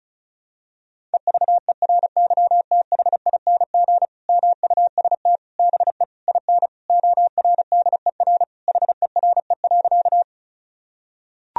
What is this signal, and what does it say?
Morse code, heard as a keyed tone, about 36 wpm, 700 Hz. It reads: EVERYTHING MUST BE IN ORDER HERE.
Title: Sentences from Top 1000 Words - Review Part 2 - 35wpm